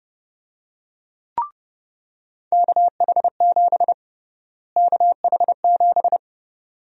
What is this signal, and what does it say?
Morse code, heard as a keyed tone, about 30 wpm, 700 Hz. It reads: KH7 KH7
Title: USA Call Sign Prefixes - Repeated 2x - 30wpm